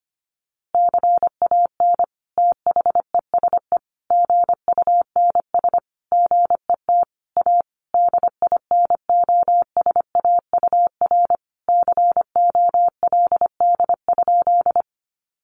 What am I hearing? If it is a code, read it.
Morse code, heard as a keyed tone, about 25 wpm, 700 Hz. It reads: CAN THESE GUNS GET A DINOSAUR COLD?
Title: A Sound Of Thunder 25wpm